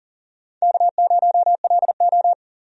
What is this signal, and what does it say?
Morse code, heard as a keyed tone, about 40 wpm, 700 Hz. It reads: K0LO
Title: USA Call Signs - 4 Characters - 40wpm